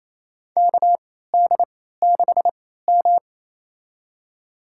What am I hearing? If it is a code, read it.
Morse code, heard as a keyed tone, about 28 wpm, spaced as for 16 wpm, 700 Hz. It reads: KD6M
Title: USA Call Signs - 4 Characters - 3x Character Spacing - 28wpm